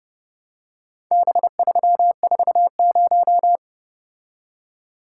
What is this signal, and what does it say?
Morse code, heard as a keyed tone, about 30 wpm, 700 Hz. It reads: B340